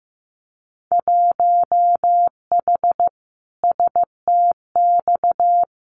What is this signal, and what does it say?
Morse code, heard as a keyed tone, about 15 wpm, 700 Hz. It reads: 1H STX
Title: Winter Field Day Exchanges 15wpm